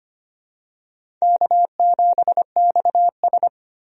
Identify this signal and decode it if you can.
Morse code, heard as a keyed tone, about 25 wpm, 700 Hz. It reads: K7XS
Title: USA Call Signs - 4 Characters - Repeated 2x - 25wpm